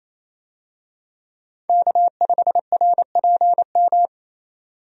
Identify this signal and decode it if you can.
Morse code, heard as a keyed tone, about 27 wpm, 700 Hz. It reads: K5RPM
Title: USA Call Signs - 5 Characters - 28wpm